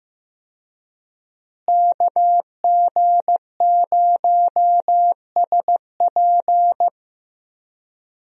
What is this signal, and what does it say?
Morse code, heard as a keyed tone, about 15 wpm, 700 Hz. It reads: KG0SP
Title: USA Call Signs - 5 Characters - Repeated 3x - 15wpm